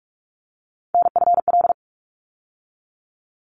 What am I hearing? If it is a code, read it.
Morse code, heard as a keyed tone, about 44 wpm, 700 Hz. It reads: NFL
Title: ARRL Sections 45wpm